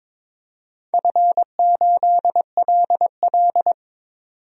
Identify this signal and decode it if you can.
Morse code, heard as a keyed tone, about 22 wpm, 700 Hz. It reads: F8LL
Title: DXCC-100 Call Signs - 4 Characters - 22wpm